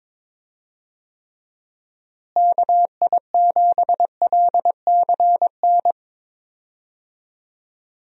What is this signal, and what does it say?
Morse code, heard as a keyed tone, about 22 wpm, 700 Hz. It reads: KI7LCN
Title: USA Call Signs - 6 Characters - Repeated 2x - 22wpm